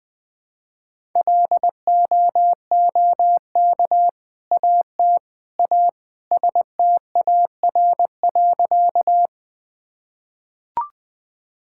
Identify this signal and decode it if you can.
Morse code, heard as a keyed tone, about 20 wpm, 700 Hz. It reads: LOOK AT A STAR.
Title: Sentences with 2-Word Phrases - 20wpm